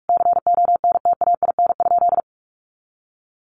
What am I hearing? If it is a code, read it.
Morse code, heard as a keyed tone, about 44 wpm, 700 Hz. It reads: CONTAIN?
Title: Sentences from Top 500 Words 45wpm